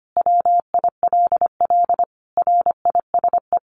Morse code, heard as a keyed tone, about 25 wpm, 700 Hz. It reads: WILL RISE
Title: Sentences from Top 900 Words 25wpm